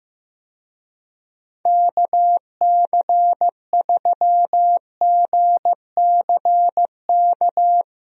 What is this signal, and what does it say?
Morse code, heard as a keyed tone, about 15 wpm, 700 Hz. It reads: KC3GCK
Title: USA Call Signs - 6 Characters - Repeated 3x - 15wpm